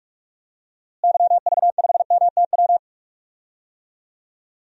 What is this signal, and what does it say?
Morse code, heard as a keyed tone, about 43 wpm, 700 Hz. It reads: YV5MTW
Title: DXCC-100 Call Signs - 6 Characters - Repeated 2x - 45wpm